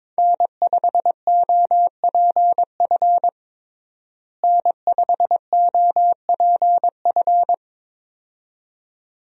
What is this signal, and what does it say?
Morse code, heard as a keyed tone, about 22 wpm, 700 Hz. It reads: N5OPF N5OPF
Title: USA Call Signs - 5 Characters - Repeated 2x - 22wpm